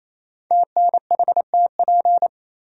Morse code, heard as a keyed tone, about 29 wpm, 700 Hz. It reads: TNHTP
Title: Code Groups - 5 Letters - 28wpm